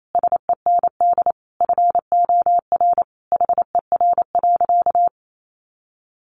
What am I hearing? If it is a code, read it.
Morse code, heard as a keyed tone, about 28 wpm, 700 Hz. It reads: SEND FOR HER.
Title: Sentences from Top 1500 Words 28wpm